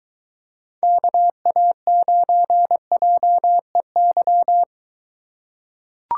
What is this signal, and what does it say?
Morse code, heard as a keyed tone, about 22 wpm, 700 Hz. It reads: KA9JEY